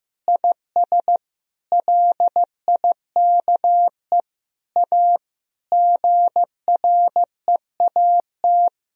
Morse code, heard as a keyed tone, about 15 wpm, 700 Hz. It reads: IS LIKE A GREAT
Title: Sentences from Top 1000 Words - Review Part 3 - 15wpm